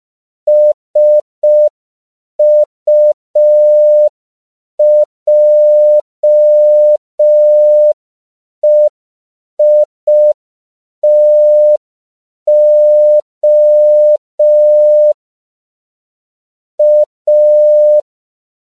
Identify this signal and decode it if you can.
Morse code, heard as a keyed tone, about 5 wpm, 600 Hz. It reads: SUJEITO A